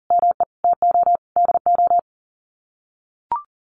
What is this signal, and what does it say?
Morse code, heard as a keyed tone, about 40 wpm, 700 Hz. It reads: ME TO DO
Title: Sets of 5 Words Encore2 40wpm